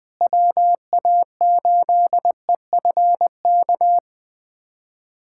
Morse code, heard as a keyed tone, about 21 wpm, 700 Hz. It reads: WA8EFK
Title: USA Callsign DE Callsign 20wpm